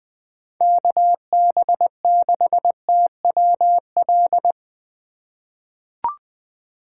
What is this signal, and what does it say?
Morse code, heard as a keyed tone, about 20 wpm, 700 Hz. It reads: KB6TWL
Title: USA Call Signs - 3x Character Spacing - 20wpm